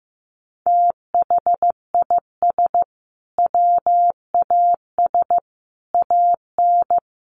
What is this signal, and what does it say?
Morse code, heard as a keyed tone, about 15 wpm, 700 Hz. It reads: THIS WAS AN
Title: Sentences from Top 1300 Words 15wpm